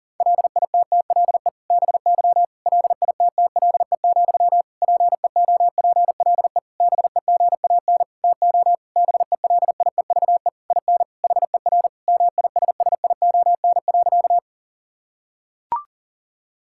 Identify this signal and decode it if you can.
Morse code, heard as a keyed tone, about 40 wpm, 700 Hz. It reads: LITTLE BY LITTLE, PEOPLE BEGAN TO BELIEVE IN HER MISSION.
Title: Sentences from Top 1600 Words 40wpm